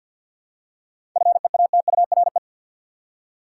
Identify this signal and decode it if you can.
Morse code, heard as a keyed tone, about 52 wpm, 700 Hz. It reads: FEATURE
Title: Seven Letter Words - 50wpm